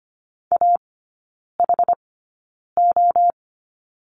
Morse code, heard as a keyed tone, about 25 wpm, spaced as for 9 wpm, 700 Hz. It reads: AHO
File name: Lesson 15e - Number 5 - Three Characters 25wpm